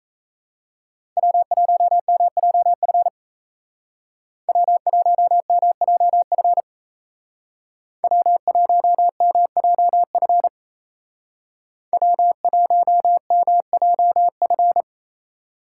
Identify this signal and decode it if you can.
Morse code, heard as a keyed tone, about 34 wpm, 700 Hz. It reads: W1MJF W1MJF W1MJF W1MJF
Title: Speed Racing - USA 1x3 Call Signs 28wpm